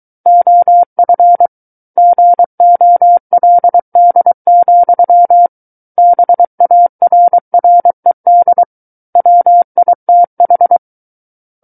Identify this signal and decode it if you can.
Morse code, heard as a keyed tone, about 23 wpm, 700 Hz. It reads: OF GOLD, BARRED WITH
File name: Book One / 14-Chapter 14 - 23 WPM